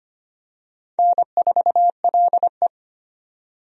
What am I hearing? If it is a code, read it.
Morse code, heard as a keyed tone, about 25 wpm, 700 Hz. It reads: N4LE
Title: USA Call Signs - 4 Characters - 25wpm